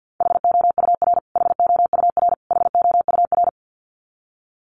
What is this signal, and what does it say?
Morse code, heard as a keyed tone, about 50 wpm, 700 Hz. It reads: HOUR HOUR HOUR
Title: Top 500 Words Repeated 50wpm